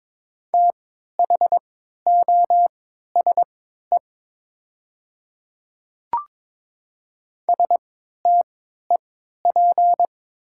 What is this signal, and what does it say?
Morse code, heard as a keyed tone, about 22 wpm, spaced as for 12 wpm, 690 Hz. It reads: THOSE STEP
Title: 3x Character Spacing - Top 500 Words - 22wpm